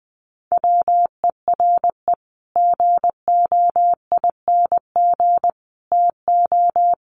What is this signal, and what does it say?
Morse code, heard as a keyed tone, about 20 wpm, 700 Hz. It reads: WERE GOING TO
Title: Sets of 3 Words 20wpm